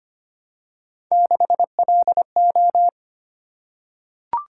Morse code, heard as a keyed tone, about 25 wpm, 700 Hz. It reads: 6LO
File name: Three Letter-Number Combinations - 25wpm